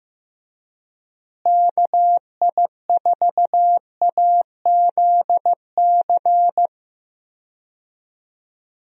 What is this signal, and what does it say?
Morse code, heard as a keyed tone, about 15 wpm, 700 Hz. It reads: KI4AZC
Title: USA Call Signs - 6 Characters - Repeated 3x - 15wpm